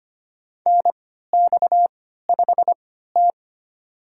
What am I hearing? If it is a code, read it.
Morse code, heard as a keyed tone, about 25 wpm, spaced as for 14 wpm, 700 Hz. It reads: NX5T
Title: USA Call Signs - 4 Characters - 3x Character Spacing - 25wpm